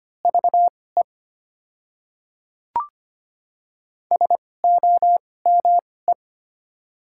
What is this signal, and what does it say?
Morse code, heard as a keyed tone, about 25 wpm, spaced as for 18 wpm, 700 Hz. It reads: VE SOME
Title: 2x Character Spacing - Top 100 Words - 25wpm